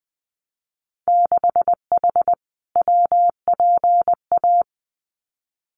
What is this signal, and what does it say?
Morse code, heard as a keyed tone, about 20 wpm, 700 Hz. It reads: 6H WPA
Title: Winter Field Day Exchanges Encore 20wpm